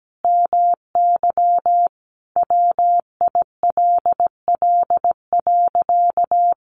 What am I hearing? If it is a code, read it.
Morse code, heard as a keyed tone, about 17 wpm, 700 Hz. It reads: MY WILL.
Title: Sentences from Top 1400 Words 17wpm